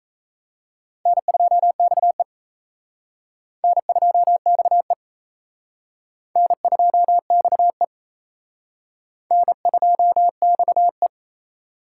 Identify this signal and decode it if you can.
Morse code, heard as a keyed tone, about 35 wpm, 700 Hz. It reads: N2XE N2XE N2XE N2XE